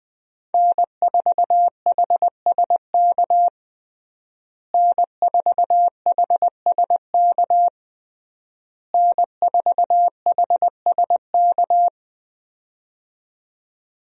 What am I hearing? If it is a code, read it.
Morse code, heard as a keyed tone, about 20 wpm, 700 Hz. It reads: N4HSK N4HSK N4HSK
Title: USA Call Signs - 5 Characters - Repeated 3x - 20wpm